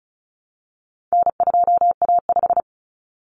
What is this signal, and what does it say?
Morse code, heard as a keyed tone, about 34 wpm, 700 Hz. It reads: N2A5
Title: Common Cut Numbers - 4-Digit 35wpm